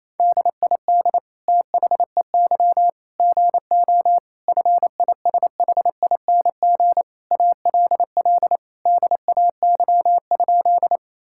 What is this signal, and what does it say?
Morse code, heard as a keyed tone, about 28 wpm, 700 Hz. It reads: DID THEY GO FISHING ALL DAY?
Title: Sentences from Top 1000 Words - Review Part 5 - 28wpm